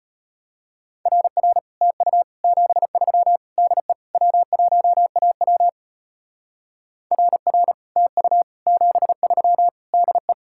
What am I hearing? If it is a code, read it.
Morse code, heard as a keyed tone, about 32 wpm, 700 Hz. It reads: RR TU 73 DE W1AW RR TU 73 DE